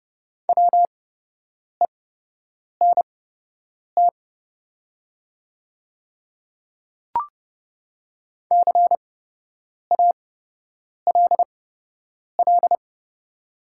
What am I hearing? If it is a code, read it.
Morse code, heard as a keyed tone, about 30 wpm, spaced as for 8 wpm, 700 Hz. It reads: WENT CALL
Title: 8x Character Spacing - Top 200 Words - 30wpm